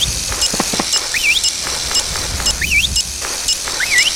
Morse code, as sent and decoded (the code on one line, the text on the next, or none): none